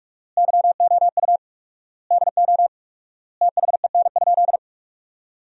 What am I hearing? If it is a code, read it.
Morse code, heard as a keyed tone, about 42 wpm, 700 Hz. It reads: YOU DO THEN?
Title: 4x Word-Spacing - Sentences from Top 100 Words - 45wpm